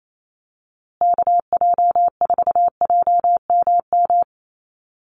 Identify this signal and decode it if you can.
Morse code, heard as a keyed tone, about 28 wpm, 700 Hz. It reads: KJ4JMM
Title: USA Call Signs Repeated 3x 28wpm